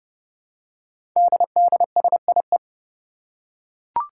Morse code, heard as a keyed tone, about 30 wpm, 700 Hz. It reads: DDSIE
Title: Five Challenging Letter-Number Combinations 30wpm